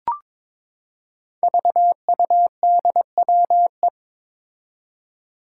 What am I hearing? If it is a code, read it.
Morse code, heard as a keyed tone, about 22 wpm, 700 Hz. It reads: VUDWE
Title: Five Challenging Letter-Number Combinations 22wpm